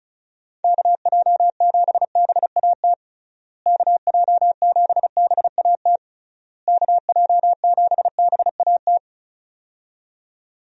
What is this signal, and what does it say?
Morse code, heard as a keyed tone, about 34 wpm, 700 Hz. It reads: KJ7BAT KJ7BAT KJ7BAT
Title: USA Call Signs - 6 Characters - Repeated 3x - 35wpm